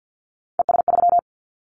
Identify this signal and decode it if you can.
Morse code, heard as a keyed tone, about 50 wpm, 700 Hz. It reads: ES3
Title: Callsign Prefixes for Top 50 DXCC - 50wpm